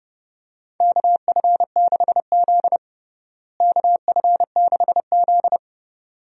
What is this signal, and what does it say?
Morse code, heard as a keyed tone, about 31 wpm, 700 Hz. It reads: KF6Z KF6Z
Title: USA Call Signs - 4 Characters - Repeated 2x - 30wpm